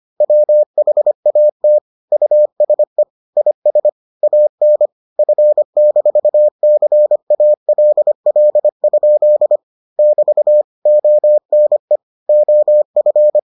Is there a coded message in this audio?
Yes: Morse code from 0.2 to 13.4 s, heard as a keyed tone, about 25 wpm, 600 Hz. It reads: WHAT USE IS AN F-CALL? <BT> ONE OF